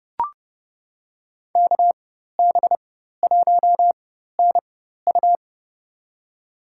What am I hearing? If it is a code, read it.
Morse code, heard as a keyed tone, about 31 wpm, spaced as for 14 wpm, 700 Hz. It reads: KB1NU
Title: USA Call Signs - 5 Characters - 4x Character Spacing - 30wpm